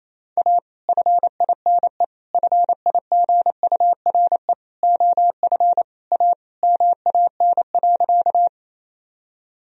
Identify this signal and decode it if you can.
Morse code, heard as a keyed tone, about 28 wpm, 700 Hz. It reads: A FINE FIGURE OF A MAN.
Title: Sentences from Top 1000 Words - Review Part 2 - 28wpm